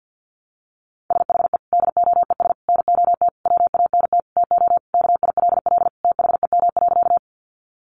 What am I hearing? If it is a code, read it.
Morse code, heard as a keyed tone, about 50 wpm, 700 Hz. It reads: SHE DOES NOT WANT TO KILL THEM.
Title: Easy Phrases 50wpm